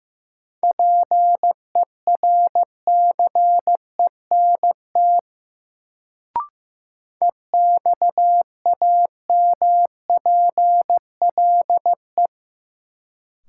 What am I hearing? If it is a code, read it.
Morse code, heard as a keyed tone, about 15 wpm, 700 Hz. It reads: PERCENT EXAMPLE
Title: Sets of 2 Words - New Word Drill - 15wpm